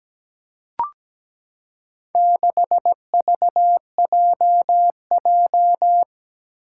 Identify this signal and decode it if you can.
Morse code, heard as a keyed tone, about 17 wpm, 700 Hz. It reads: 6VJJ
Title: Four Challenging Letter-Number Combinations 17wpm